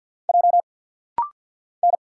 Morse code, heard as a keyed tone, about 50 wpm, 700 Hz. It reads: J N
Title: Single Letters - Rapid-Fire 50wpm